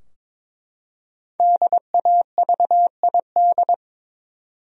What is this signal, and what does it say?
Morse code, heard as a keyed tone, about 23 wpm, 700 Hz. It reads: DAVID
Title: USA Common Names - Max 5 Letters - 22wpm